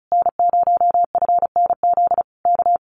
Morse code, heard as a keyed tone, about 34 wpm, 700 Hz. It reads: N0FNZ K
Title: Common QSO Phrases Encore 35wpm